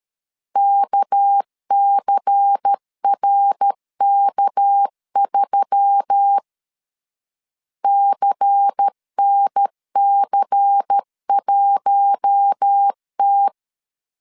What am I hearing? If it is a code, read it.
Morse code, heard as a keyed tone, about 13 wpm, 800 Hz. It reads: KCRK3 CNC1T
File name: CW / DET / 1 6 1500C S A 12wpm 3440